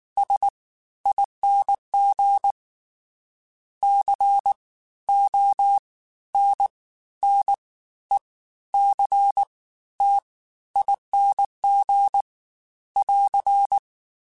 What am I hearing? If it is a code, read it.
Morse code, heard as a keyed tone, about 19 wpm, spaced as for 13 wpm, 800 Hz. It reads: SING CONNECTING <AR>
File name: Special INGs-mp3s / Special INGs-mp3s / Fast ing @18